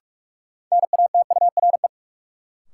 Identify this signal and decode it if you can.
Morse code, heard as a keyed tone, about 46 wpm, 700 Hz. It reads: NATURE
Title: Sentences from Top 600 Words - New Word Drill - 45wpm